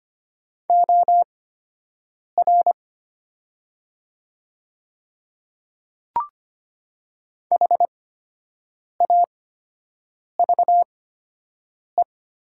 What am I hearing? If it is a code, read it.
Morse code, heard as a keyed tone, about 25 wpm, spaced as for 7 wpm, 700 Hz. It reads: OR HAVE
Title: 8x Character Spacing - Top 100 Words - 25wpm